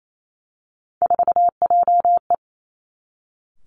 Morse code, heard as a keyed tone, about 28 wpm, 700 Hz. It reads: 4JE
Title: Three Challenging Letter-Number Combinations 28wpm